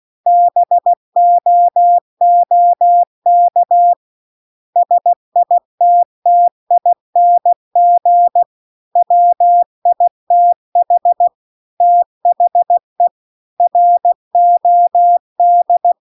Morse code, heard as a keyed tone, about 16 wpm, 700 Hz. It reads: BOOK SITTING WITH THE ROD